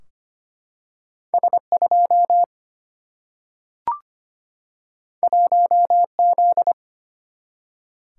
Morse code, heard as a keyed tone, about 25 wpm, 700 Hz. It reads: S2 1Z
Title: Single-Letter Single-Number 25wpm